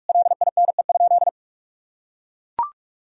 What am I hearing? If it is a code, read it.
Morse code, heard as a keyed tone, about 42 wpm, 700 Hz. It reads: LINE?
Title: Sentences from Top 1000 Words - Review Part 1 - 45wpm